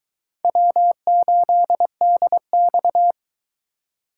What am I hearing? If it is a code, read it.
Morse code, heard as a keyed tone, about 22 wpm, 700 Hz. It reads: W8DX